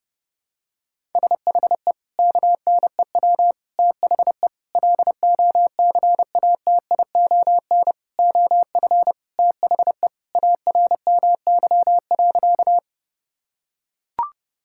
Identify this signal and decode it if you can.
Morse code, heard as a keyed tone, about 29 wpm, 700 Hz. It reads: SHE KNEW THE LOCATION OF THE ARMY.
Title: Sentences from Top 1000 Words - Review Part 6 - 30wpm